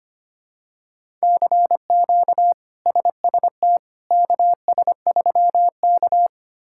Morse code, heard as a keyed tone, about 25 wpm, 700 Hz. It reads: CQ SST KS3K
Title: K1USN SST QSO Practice - 25wpm